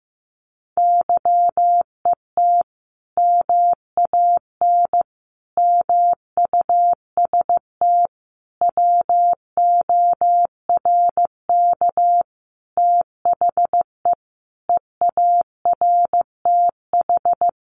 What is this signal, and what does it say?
Morse code, heard as a keyed tone, about 15 wpm, 700 Hz. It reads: YET MAN MUST WORK THE EARTH
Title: Sentences from Top 1500 Words 15wpm